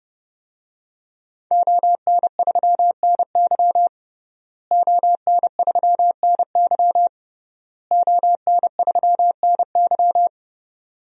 Morse code, heard as a keyed tone, about 29 wpm, 700 Hz. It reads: ON3NY ON3NY ON3NY
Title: DXCC-100 Call Signs - 5 Characters - Repeated 3x - 30wpm